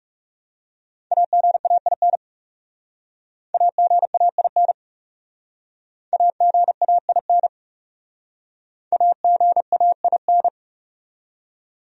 Morse code, heard as a keyed tone, about 37 wpm, 700 Hz. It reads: AGAIN AGAIN AGAIN AGAIN
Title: Speed-Racing Top 300 Words 30wpm